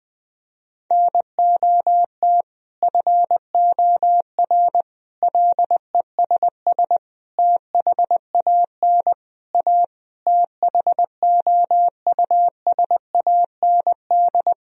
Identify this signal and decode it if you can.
Morse code, heard as a keyed tone, about 20 wpm, 700 Hz. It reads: NOT FOR LESS THAN A THOUSAND
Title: Sentences from Top 1000 Words - Review Part 3 - 20wpm